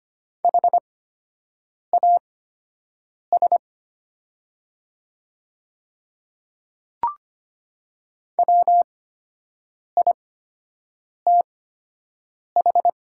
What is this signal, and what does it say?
Morse code, heard as a keyed tone, about 25 wpm, spaced as for 7 wpm, 700 Hz. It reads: HAS WITH
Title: 8x Character Spacing - Top 100 Words - 25wpm